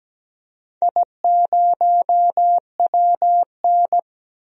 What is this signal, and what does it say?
Morse code, heard as a keyed tone, about 17 wpm, 700 Hz. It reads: I0WN
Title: DXCC-100 Call Signs - 4 Characters - Repeated 3x - 17wpm